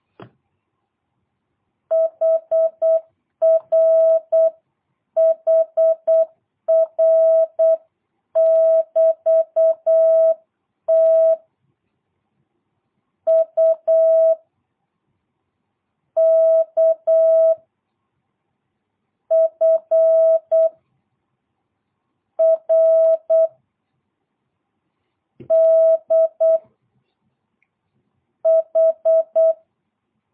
Morse code, heard as a keyed tone, about 8 wpm, 650 Hz. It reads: HR HR <BT> T U K F R D H
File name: jh8cbh / 2022cw